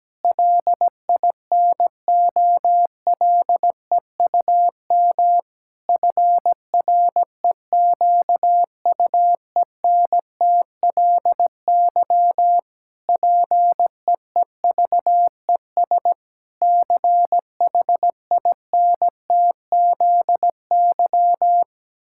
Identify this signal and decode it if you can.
Morse code, heard as a keyed tone, about 17 wpm, 700 Hz. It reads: LINOLEUM FREQUENTLY PEEVES CHINTZY
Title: Pangram Challenge v2 - 17wpm